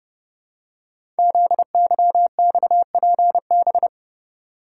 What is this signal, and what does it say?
Morse code, heard as a keyed tone, about 30 wpm, 700 Hz. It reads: ZYXPB
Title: Code Groups - 5 Letter:Numbers - 30wpm